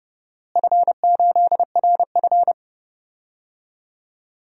Morse code, heard as a keyed tone, about 32 wpm, 700 Hz. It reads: F8RF
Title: DXCC-100 Call Signs - 4 Characters - Repeated 2x - 30wpm